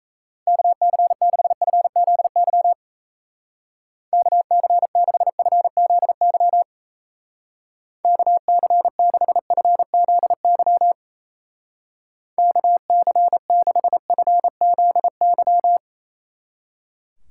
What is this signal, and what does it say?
Morse code, heard as a keyed tone, about 33 wpm, 700 Hz. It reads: KC6FZY KC6FZY KC6FZY KC6FZY
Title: Speed Racing - USA Call Signs 28wpm